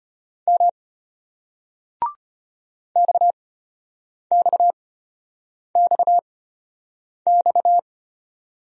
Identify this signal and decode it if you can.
Morse code, heard as a keyed tone, about 32 wpm, spaced as for 8 wpm, 700 Hz. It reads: M XXXX